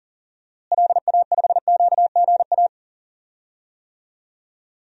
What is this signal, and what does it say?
Morse code, heard as a keyed tone, about 40 wpm, 700 Hz. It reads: LA5QGA
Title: DXCC-100 Call Signs - 6 Characters - Repeated 2x - 40wpm